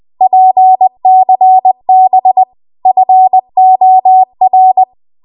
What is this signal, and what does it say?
Morse code, heard as a keyed tone, about 20 wpm, 750 Hz. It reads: PCB FOR
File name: Archive / 20 WPM / 20WPM